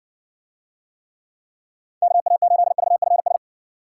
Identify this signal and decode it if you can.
Morse code, heard as a keyed tone, about 55 wpm, 700 Hz. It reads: KA8VFS